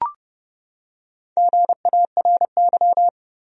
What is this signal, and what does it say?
Morse code, heard as a keyed tone, about 30 wpm, 700 Hz. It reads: GARY
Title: USA Common Names - Max 5 Letters - 30wpm